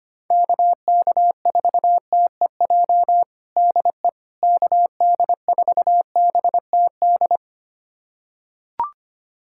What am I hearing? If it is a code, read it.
Morse code, heard as a keyed tone, about 25 wpm, 700 Hz. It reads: KK4TEJ DE KD4BTD
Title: USA Callsign DE Callsign - 6 Characters - 25wpm